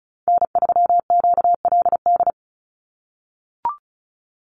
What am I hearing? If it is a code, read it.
Morse code, heard as a keyed tone, about 34 wpm, 700 Hz. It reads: N3QLD
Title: USA Call Signs 35wpm